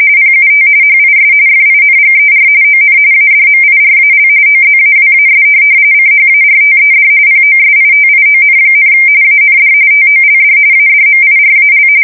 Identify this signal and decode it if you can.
RTTY, about 45.45 baud. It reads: PUT HIS FINGER ON ADMIRAL OZAWAS CHARTED POSITION, AND TOLD ADMIRAL
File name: fsk170 / USS-New-Jersey